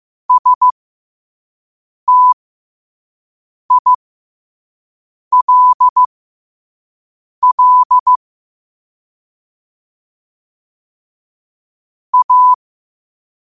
Morse code, heard as a keyed tone, about 15 wpm, spaced as for 5 wpm, 1000 Hz. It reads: STILL A
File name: qotdfiles / 5wpm